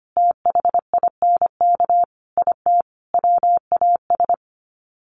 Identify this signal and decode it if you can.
Morse code, heard as a keyed tone, about 25 wpm, 700 Hz. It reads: THINK IT WAS
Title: Sets of 4 Words 25wpm